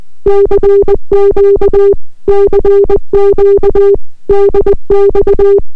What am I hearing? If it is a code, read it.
Morse code, heard as a keyed tone, about 20 wpm, 400 Hz. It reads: CQ CQ DX